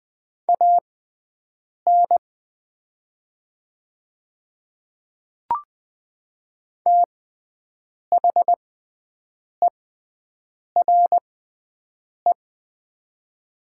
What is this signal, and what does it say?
Morse code, heard as a keyed tone, about 20 wpm, spaced as for 7 wpm, 700 Hz. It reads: AN THERE
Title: 6x Character Spacing - Top 100 Words - 20wpm